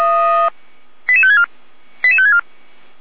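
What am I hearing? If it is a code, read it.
Five-tone selective call (ZVEI), heard as quick runs of five tones, tones of about 70 ms. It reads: 88564 88564
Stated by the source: ZVEI five-tone selective call received on SDR.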